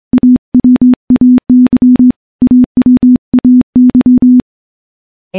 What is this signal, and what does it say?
Morse code, heard as a keyed tone, about 24 wpm, 250 Hz. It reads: AWAY AWAY
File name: morse-training-25WPM